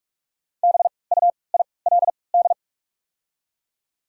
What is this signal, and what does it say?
Morse code, heard as a keyed tone, about 43 wpm, spaced as for 25 wpm, 690 Hz. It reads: BUILD
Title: 3x Character Spacing - Top 200 Words - 45wpm